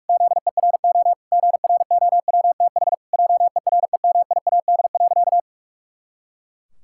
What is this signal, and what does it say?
Morse code, heard as a keyed tone, about 46 wpm, 700 Hz. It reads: ZERO GROWTH JEREMIAD.
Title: Pangram Challenge v2 - 45wpm